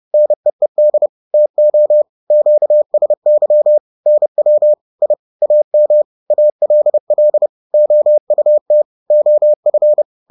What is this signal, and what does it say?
Morse code, heard as a keyed tone, about 30 wpm, 600 Hz. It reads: NEED TO QSY NW I AM ALL OUT OF